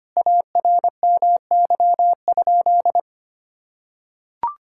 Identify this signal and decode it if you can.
Morse code, heard as a keyed tone, about 25 wpm, 700 Hz. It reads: ARMY?
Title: Sentences from Top 1000 Words - Review Part 5 - 25wpm